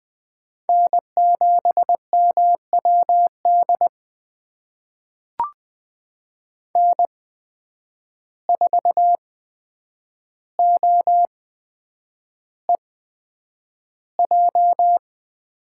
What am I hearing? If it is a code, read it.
Morse code, heard as a keyed tone, about 20 wpm, spaced as for 9 wpm, 700 Hz. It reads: N7MWD N4OEJ
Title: USA Call Signs - 8x Character Spacing - 20wpm